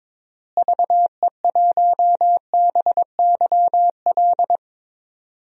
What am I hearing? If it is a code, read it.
Morse code, heard as a keyed tone, about 22 wpm, 700 Hz. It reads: VE1BYL
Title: DXCC-100 Call Signs - 6 Characters - Repeated 3x - 22wpm